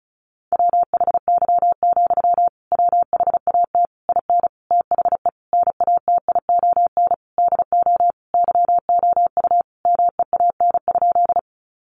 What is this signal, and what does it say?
Morse code, heard as a keyed tone, about 34 wpm, 700 Hz. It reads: WHY, WHAT IN THE NATION DO YOU MEAN?
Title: Sentences from Top 900 Words 35wpm